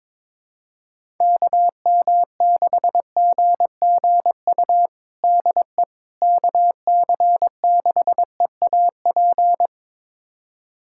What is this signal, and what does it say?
Morse code, heard as a keyed tone, about 21 wpm, 700 Hz. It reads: KM6GGU DE KC6EAP
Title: USA Callsign DE Callsign 22wpm